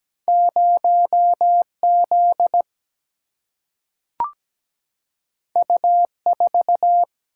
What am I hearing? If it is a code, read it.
Morse code, heard as a keyed tone, about 17 wpm, 700 Hz. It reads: 0 Z U 4
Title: Single-Letter Single-Number 17wpm